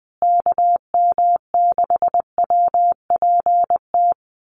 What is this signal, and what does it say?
Morse code, heard as a keyed tone, about 20 wpm, 700 Hz. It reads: KM6WPT
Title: USA Call Signs Repeated 3x 20wpm